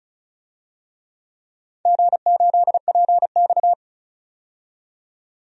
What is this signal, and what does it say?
Morse code, heard as a keyed tone, about 36 wpm, 700 Hz. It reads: G8PX
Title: DXCC-100 Call Signs - 4 Characters - 35wpm